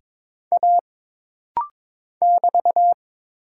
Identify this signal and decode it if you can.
Morse code, heard as a keyed tone, about 22 wpm, 700 Hz. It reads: A <BT>
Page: This rapid-fire timing is geared to push your ICR (Instant Character Recognition) skills.